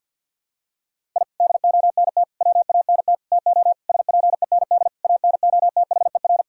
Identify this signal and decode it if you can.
Morse code, heard as a keyed tone, about 50 wpm, 700 Hz. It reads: I DONT WANT TO SPEND ANOTHER
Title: Sentences from Top 1000 Words - Review Part 5 - 50wpm